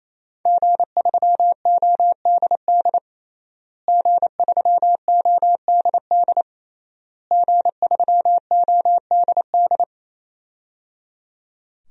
Morse code, heard as a keyed tone, about 27 wpm, 700 Hz. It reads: G3ODD G3ODD G3ODD
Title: DXCC-100 Call Signs - 5 Characters - Repeated 3x - 28wpm